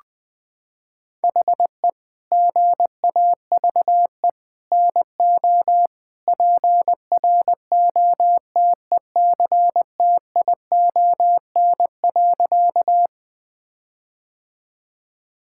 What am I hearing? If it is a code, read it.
Morse code, heard as a keyed tone, about 20 wpm, 700 Hz. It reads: HE GAVE NO PROTECTION.
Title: Sentences from Top 1600 Words 20wpm